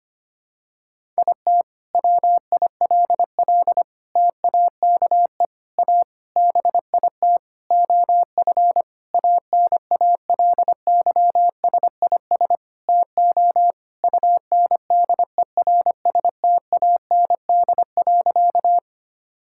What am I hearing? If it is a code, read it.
Morse code, heard as a keyed tone, about 25 wpm, 700 Hz. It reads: IT WILL TAKE A BIT OF ANALYSIS TO UNDERSTAND.
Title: Sentences from Top 1000 Words - Review Part 5 - 25wpm